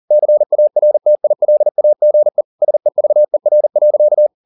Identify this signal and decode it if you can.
Morse code, heard as a keyed tone, about 40 wpm, 600 Hz. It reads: CARTILAGE SEVER.
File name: the old man and the sea 2 40wpm